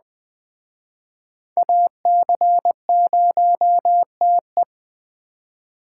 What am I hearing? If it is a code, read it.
Morse code, heard as a keyed tone, about 20 wpm, 700 Hz. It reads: AC0TE
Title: Speed Racing - USA Call Signs 17wpm